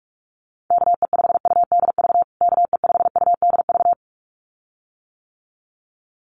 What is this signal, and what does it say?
Morse code, heard as a keyed tone, about 44 wpm, 700 Hz. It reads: KE5UDV KE5UDV
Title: USA Call Signs Repeated 2x 45wpm